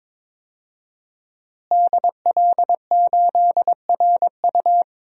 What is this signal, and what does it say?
Morse code, heard as a keyed tone, about 22 wpm, 700 Hz. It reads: DL8RU
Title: DXCC-100 Call Signs - 5 Characters - Repeated 2x - 22wpm